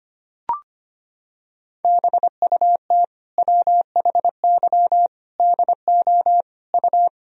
Morse code, heard as a keyed tone, about 25 wpm, 700 Hz. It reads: BUT WHY DO U
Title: Sentences from Top 200 Words and Abbreviations 25wpm